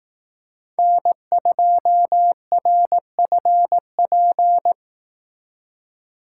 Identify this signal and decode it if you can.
Morse code, heard as a keyed tone, about 18 wpm, 700 Hz. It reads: N2RFP
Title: Speed Racing - USA 1x3 Call Signs 15wpm